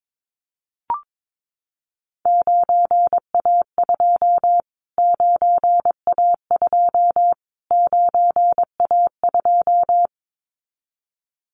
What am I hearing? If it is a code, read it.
Morse code, heard as a keyed tone, about 22 wpm, 700 Hz. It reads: 9A2 9A2 9A2
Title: Callsign Prefixes for Top 25 DXCC - Repeated 3x - 22wpm